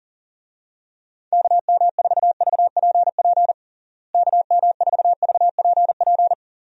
Morse code, heard as a keyed tone, about 40 wpm, 700 Hz. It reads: KM4VPP KM4VPP
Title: USA Call Signs - 6 Characters - Repeated 3x - 40wpm